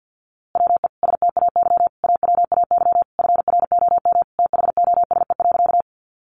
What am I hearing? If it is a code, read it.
Morse code, heard as a keyed tone, about 50 wpm, 700 Hz. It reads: WE STAY AWAY FROM THOSE.
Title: A Sound Of Thunder 50wpm